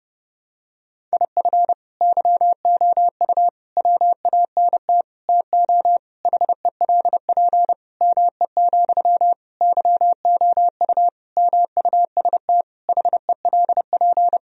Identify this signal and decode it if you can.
Morse code, heard as a keyed tone, about 30 wpm, 700 Hz. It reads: IF YOU WANT TO HELP ME, YOU MUST HELP
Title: Sentences from Top 1000 Words - Review Part 1 - 30wpm